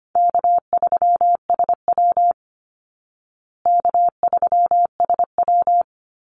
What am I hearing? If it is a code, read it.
Morse code, heard as a keyed tone, about 25 wpm, 700 Hz. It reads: K3SW K3SW
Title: Easy Call Signs Repeated 2x - 25wpm